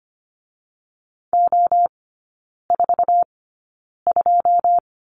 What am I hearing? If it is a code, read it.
Morse code, heard as a keyed tone, about 25 wpm, spaced as for 9 wpm, 690 Hz. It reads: O42
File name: Lesson 14e - Number 2 - Three Characters 25wpm